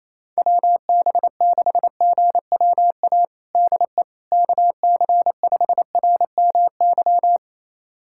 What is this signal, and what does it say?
Morse code, heard as a keyed tone, about 28 wpm, 700 Hz. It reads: WB6GWA DE KC5RMY
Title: USA Callsign DE Callsign 28wpm